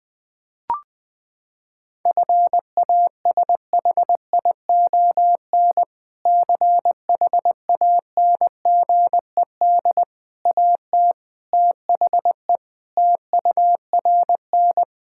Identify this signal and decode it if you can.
Morse code, heard as a keyed tone, about 21 wpm, 700 Hz. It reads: FASHION CHANGED AT THE TURN
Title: Sentences with 5-Word Phrases - 20wpm